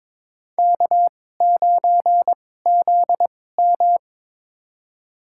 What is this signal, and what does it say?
Morse code, heard as a keyed tone, about 23 wpm, spaced as for 16 wpm, 700 Hz. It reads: K9ZM
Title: USA Call Signs - 4 Characters - 2x Character Spacing - 22wpm